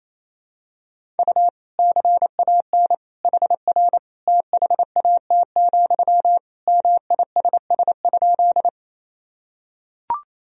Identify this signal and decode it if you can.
Morse code, heard as a keyed tone, about 27 wpm, 700 Hz. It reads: U CAN HR THAT, MISS?
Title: Sentences from Top 500 Words and Abbreviations 28wpm